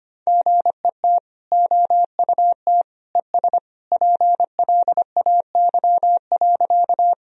Morse code, heard as a keyed tone, about 25 wpm, 700 Hz. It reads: GET OUT ES PLAY.
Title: Sentences from Top 200 Words and Abbreviations 25wpm